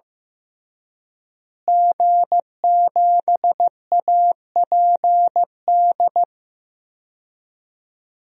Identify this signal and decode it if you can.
Morse code, heard as a keyed tone, about 15 wpm, 700 Hz. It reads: G7APD
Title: DXCC-100 Call Signs - 5 Characters - Repeated 3x - 15wpm